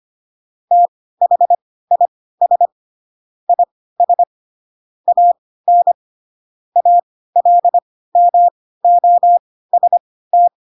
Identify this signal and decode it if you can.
Morse code, heard as a keyed tone, about 25 wpm, spaced as for 16 wpm, 700 Hz. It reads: THIS IS AN ALMOST
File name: Pinocchio / Pinocchio ch3 10 25 ik1hge